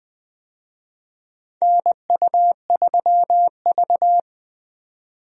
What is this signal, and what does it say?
Morse code, heard as a keyed tone, about 21 wpm, 700 Hz. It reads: NU3V
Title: USA Call Signs - 4 Characters - Repeated 3x - 20wpm